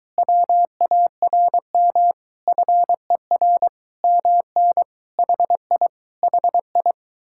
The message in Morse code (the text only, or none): WARM FER MN HI HI